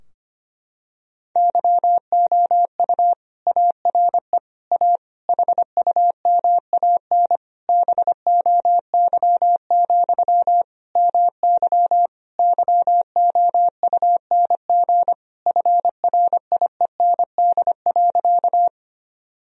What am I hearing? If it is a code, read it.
Morse code, heard as a keyed tone, about 25 wpm, 700 Hz. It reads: YOU ARE A HUMAN BOY, MY YOUNG FRIEND.
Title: Sentences from Top 1000 Words - Review Part 3 - 25wpm